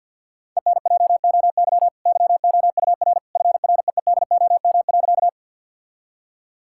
Morse code, heard as a keyed tone, about 50 wpm, 700 Hz. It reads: ENJOY YOUR FREEDOM.
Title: Sentences from Top 1000 Words - Review Part 5 - 50wpm